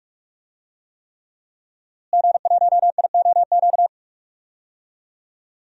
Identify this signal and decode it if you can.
Morse code, heard as a keyed tone, about 46 wpm, 700 Hz. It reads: G1IOQ
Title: DXCC-100 Call Signs - 5 Characters - Repeated 3x - 45wpm